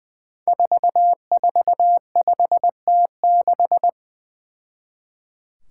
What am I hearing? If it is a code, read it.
Morse code, heard as a keyed tone, about 20 wpm, 700 Hz. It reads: 445T6
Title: Five Challenging Letter-Number Combinations 20wpm